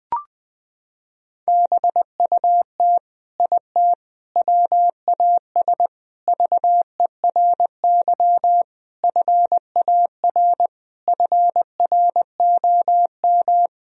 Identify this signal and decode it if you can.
Morse code, heard as a keyed tone, about 20 wpm, 700 Hz. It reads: BUT IT WAS VERY FAR FROM
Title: Sentences from Top 1000 Words - Review Part 2 - 20wpm